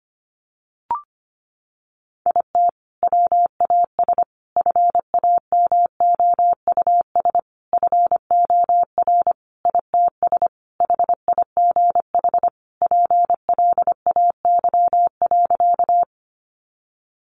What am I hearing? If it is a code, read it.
Morse code, heard as a keyed tone, about 25 wpm, 700 Hz. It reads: IT WAS FAMOUS FOR ITS HIGH PLAY.
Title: Sentences from Top 1500 Words 25wpm